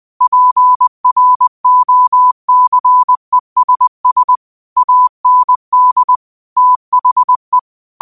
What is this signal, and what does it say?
Morse code, heard as a keyed tone, about 20 wpm, 1000 Hz. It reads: PROCESS AND THE
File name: qotdfiles / 20wpm